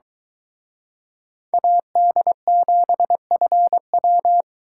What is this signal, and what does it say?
Morse code, heard as a keyed tone, about 23 wpm, 700 Hz. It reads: AD7FW